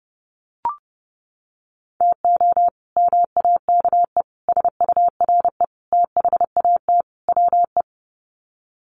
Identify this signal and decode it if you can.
Morse code, heard as a keyed tone, about 30 wpm, 700 Hz. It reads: TO MAKE SURE THAT WE
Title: Sets of 5 Words Encore 30wpm